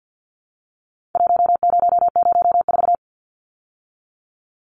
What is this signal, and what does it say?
Morse code, heard as a keyed tone, about 50 wpm, 700 Hz. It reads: 1004
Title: Numbers - Contest Style - 4-Digit 50wpm